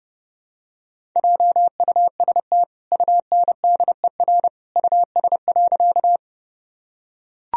Morse code, heard as a keyed tone, about 30 wpm, 700 Hz. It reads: JUST UNDER US.
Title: Sentences from Top 1000 Words - Review Part 1 - 30wpm